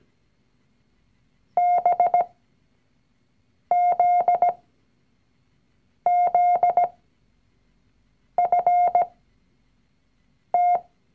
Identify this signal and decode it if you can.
Morse code, heard as a keyed tone, about 17 wpm, spaced as for 5 wpm, 710 Hz. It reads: BZZFT
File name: CW-Online-Kurs / Anfänger / Lektion 019